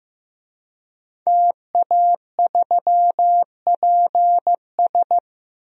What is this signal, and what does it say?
Morse code, heard as a keyed tone, about 14 wpm, 700 Hz. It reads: TA3PS
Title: DXCC-100 Call Signs - 5 Characters - Repeated 3x - 15wpm